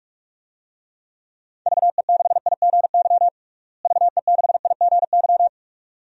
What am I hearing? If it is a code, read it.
Morse code, heard as a keyed tone, about 46 wpm, 690 Hz. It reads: VE6IGY VE6IGY
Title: DXCC-100 Call Signs - 6 Characters - Repeated 2x - 45wpm